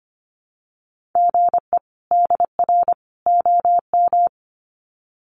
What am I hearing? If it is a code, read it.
Morse code, heard as a keyed tone, about 25 wpm, 700 Hz. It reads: GE DR OM
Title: Common QSO Phrases 25wpm